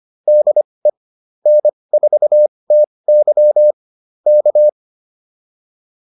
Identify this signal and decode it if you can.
Morse code, heard as a keyed tone, about 25 wpm, spaced as for 20 wpm, 600 Hz. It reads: DE N4TY K
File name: QSO109a 20